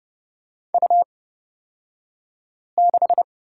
Morse code, heard as a keyed tone, about 31 wpm, spaced as for 5 wpm, 700 Hz. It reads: U6
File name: Lesson 37d - Letter x - Two Characters 30wpm